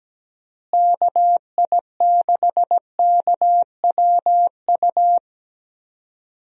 Morse code, heard as a keyed tone, about 17 wpm, 700 Hz. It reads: KI6KWU
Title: USA Call Signs - 6 Characters - Repeated 2x - 17wpm